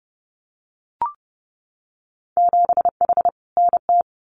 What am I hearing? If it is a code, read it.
Morse code, heard as a keyed tone, about 30 wpm, 700 Hz. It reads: 7H NT
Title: Winter Field Day Exchanges Encore 30wpm